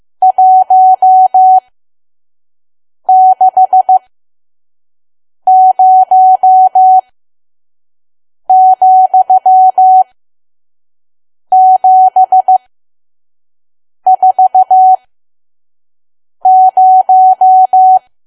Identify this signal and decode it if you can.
Morse code, heard as a keyed tone, about 15 wpm, spaced as for 5 wpm, 750 Hz. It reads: 160,740
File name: Archive / 5 WPM files / PM